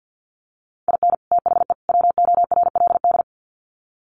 Morse code, heard as a keyed tone, about 50 wpm, 700 Hz. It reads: IN THE WORLD
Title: Sets of 3 Words 50wpm